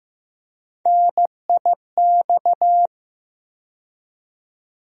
Morse code, heard as a keyed tone, about 15 wpm, 700 Hz. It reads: NIX
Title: USA Call Sign Postfixes - 3 Letters 15wpm